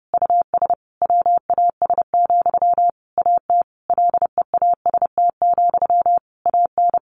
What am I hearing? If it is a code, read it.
Morse code, heard as a keyed tone, about 30 wpm, 700 Hz. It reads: US WAS, AT LEAST, AN
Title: Sentences from Top 1400 Words 30wpm